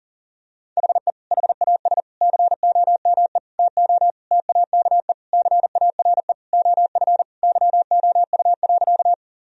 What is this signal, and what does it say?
Morse code, heard as a keyed tone, about 40 wpm, 700 Hz. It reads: HE HAS COME TO TAKE CARE OF YOU.
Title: Sentences from Top 1000 Words - Review Part 1 - 40wpm